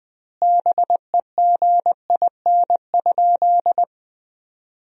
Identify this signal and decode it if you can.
Morse code, heard as a keyed tone, about 20 wpm, 700 Hz. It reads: BEGIN?
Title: 2x Word-Spacing - Sentences from Top 300 Words - 20wpm